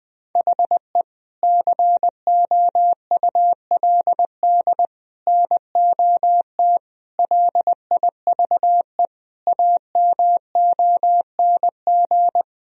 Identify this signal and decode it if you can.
Morse code, heard as a keyed tone, about 21 wpm, 700 Hz. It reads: HE COULD NOT LIVE AMONG
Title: Sentences from Top 1000 Words - Review Part 3 - 20wpm